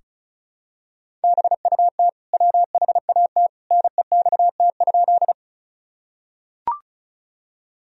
Morse code, heard as a keyed tone, about 34 wpm, 700 Hz. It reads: BUT WHAT NEXT?
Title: Sentences from Top 1000 Words - Review Part 1 - 35wpm